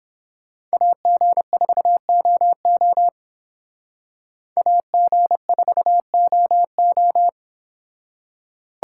Morse code, heard as a keyed tone, about 28 wpm, 700 Hz. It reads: AG4OO AG4OO